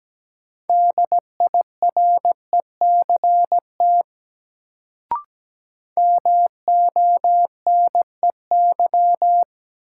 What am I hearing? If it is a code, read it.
Morse code, heard as a keyed tone, about 17 wpm, 700 Hz. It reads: DIRECT MONEY
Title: Sentences from Top 400 Words - New Word Drill - 17wpm